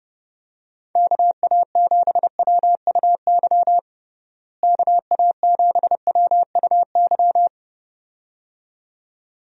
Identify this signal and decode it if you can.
Morse code, heard as a keyed tone, about 32 wpm, 700 Hz. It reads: KA7WUY KA7WUY
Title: USA Call Signs - 6 Characters - Repeated 2x - 30wpm